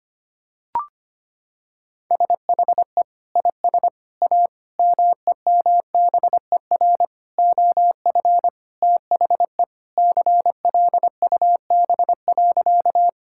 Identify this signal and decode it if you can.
Morse code, heard as a keyed tone, about 25 wpm, 700 Hz. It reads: SHE IS A MEMBER OF THE CLUB.
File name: practice / 25wpm / Sentences with 4-Word Phrases - 25wpm